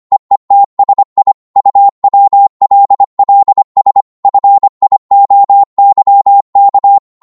Morse code, heard as a keyed tone, about 25 wpm, 800 Hz. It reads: EETSI UWLLS FIOYK